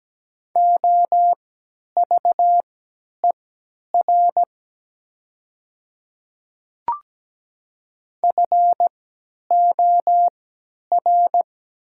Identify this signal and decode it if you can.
Morse code, heard as a keyed tone, about 17 wpm, spaced as for 10 wpm, 700 Hz. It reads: OVER FOR
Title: 3x Character Spacing - Top 100 Words - 17wpm